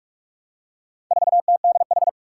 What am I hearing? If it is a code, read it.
Morse code, heard as a keyed tone, about 43 wpm, 700 Hz. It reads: 4TDH
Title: Four Challenging Letter-Number Combinations 45wpm